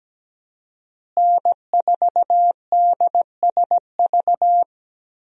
Morse code, heard as a keyed tone, about 17 wpm, 700 Hz. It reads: N4DSV
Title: USA Call Signs - 5 Characters - Repeated 2x - 17wpm